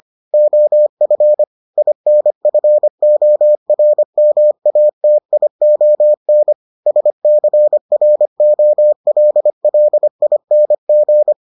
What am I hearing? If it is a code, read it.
Morse code, heard as a keyed tone, about 25 wpm, 600 Hz. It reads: OF INFORMATION SCROLLING